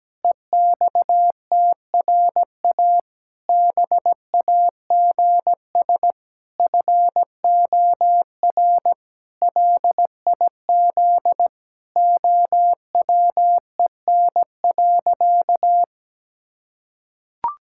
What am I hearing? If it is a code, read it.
Morse code, heard as a keyed tone, about 17 wpm, 700 Hz. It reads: EXTRA BAGS FOR LIZ OWEN.
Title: Pangram Challenge v2 - 17wpm